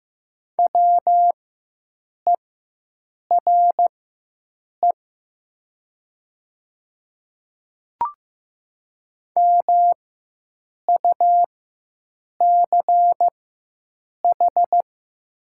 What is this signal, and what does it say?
Morse code, heard as a keyed tone, about 15 wpm, spaced as for 7 wpm, 700 Hz. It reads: WERE MUCH
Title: 4x Character Spacing - Top 200 Words - 15wpm